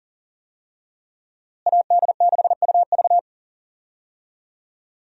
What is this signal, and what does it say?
Morse code, heard as a keyed tone, about 39 wpm, 700 Hz. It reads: AD6UV
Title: USA Call Signs - 5 Characters - Repeated 3x - 40wpm